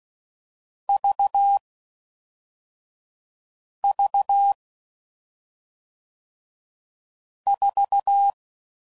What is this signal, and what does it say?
Morse code, heard as a keyed tone, about 16 wpm, 800 Hz. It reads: V V 4